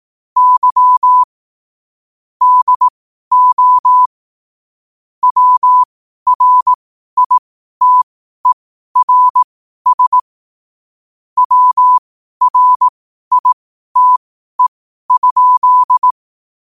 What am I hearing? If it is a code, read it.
Morse code, heard as a keyed tone, about 18 wpm, spaced as for 12 wpm, 1000 Hz. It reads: Y DO WRITERS WRITE?
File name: qotdfiles / 12wpm